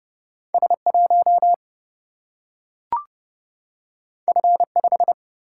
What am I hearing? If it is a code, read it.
Morse code, heard as a keyed tone, about 30 wpm, 700 Hz. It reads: S1 F5
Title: Single-Letter Single-Number 30wpm